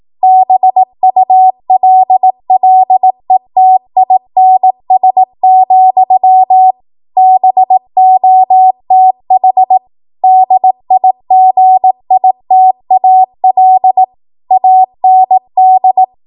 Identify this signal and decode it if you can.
Morse code, heard as a keyed tone, about 18 wpm, 750 Hz. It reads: BULLETINS, BOTH DIGITAL AND
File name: Archive / 18 WPM / 18WPM